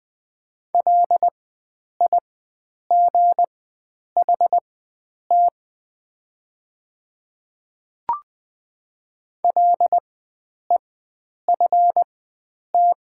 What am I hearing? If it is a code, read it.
Morse code, heard as a keyed tone, about 20 wpm, spaced as for 9 wpm, 700 Hz. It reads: LIGHT LEFT
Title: 4x Character Spacing - Top 300 Words - 20wpm